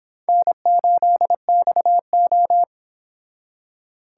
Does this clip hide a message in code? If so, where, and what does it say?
Yes: Morse code from 0.3 to 2.6 s, heard as a keyed tone, about 25 wpm, 700 Hz. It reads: N8XO